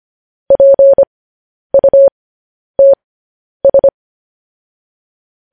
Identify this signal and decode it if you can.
Morse code, heard as a keyed tone, about 25 wpm, spaced as for 10 wpm, 550 Hz. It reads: PUTS
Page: News Headlines in Morse Code at 10 WPM